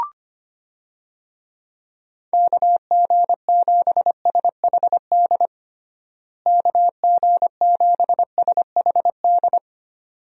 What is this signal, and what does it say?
Morse code, heard as a keyed tone, about 25 wpm, 700 Hz. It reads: KG7SHD KG7SHD
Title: USA Call Signs - 6 Characters - Repeated 2x - 25wpm